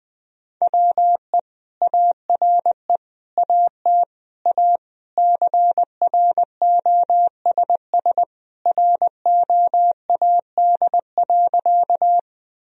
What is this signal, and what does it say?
Morse code, heard as a keyed tone, about 20 wpm, 700 Hz. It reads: WE ARE AT A CROSS ROAD.
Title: Sentences from Top 1000 Words - Review Part 2 - 20wpm